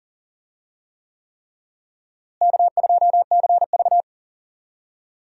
Morse code, heard as a keyed tone, about 40 wpm, 700 Hz. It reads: K2CV
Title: USA Call Signs - 4 Characters - Repeated 3x - 40wpm